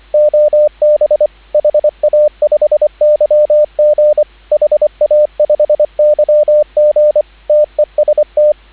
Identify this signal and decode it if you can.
Morse code, heard as a keyed tone, about 25 wpm, 600 Hz, with radio noise. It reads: OB HA5YG HA5YG TEST
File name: CW / 2017 HA-OB HA5YG